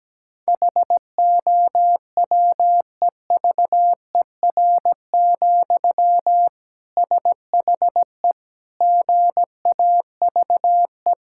Morse code, heard as a keyed tone, about 17 wpm, 700 Hz. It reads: HOWEVER, SHE GAVE
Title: Sentences from Top 1000 Words - Review Part 6 - 17wpm